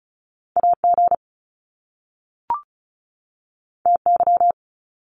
Morse code, heard as a keyed tone, about 36 wpm, 700 Hz. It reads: AG TY
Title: Top Two Letter Combinations in Words 35wpm